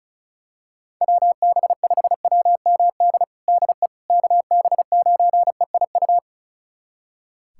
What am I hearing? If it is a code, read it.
Morse code, heard as a keyed tone, about 36 wpm, 700 Hz. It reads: WB5WMD DE KB9EIU
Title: USA Callsign DE Callsign - 6 Characters - 35wpm